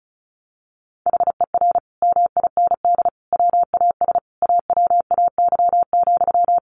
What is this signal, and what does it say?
Morse code, heard as a keyed tone, about 34 wpm, 700 Hz. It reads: HER MIND WAS AWAY,
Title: Sentences from Top 1500 Words 35wpm